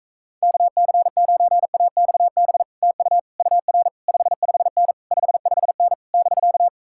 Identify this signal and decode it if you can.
Morse code, heard as a keyed tone, about 45 wpm, 700 Hz. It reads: KC9AXB TU UR 55N 55N <BK>